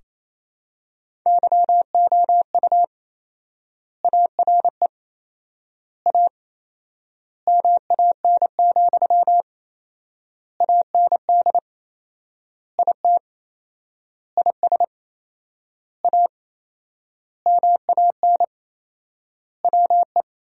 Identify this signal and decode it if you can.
Morse code, heard as a keyed tone, about 28 wpm, 700 Hz. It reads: YOU ARE A MAN, AND IT IS A MAN WE
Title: 4x Word-Spacing - Sentences from Top 200 Words - 28wpm